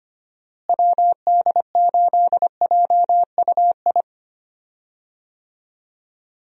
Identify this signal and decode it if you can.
Morse code, heard as a keyed tone, about 25 wpm, 700 Hz. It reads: WD8JUI
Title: USA Call Signs - 6 Characters - Repeated 3x - 25wpm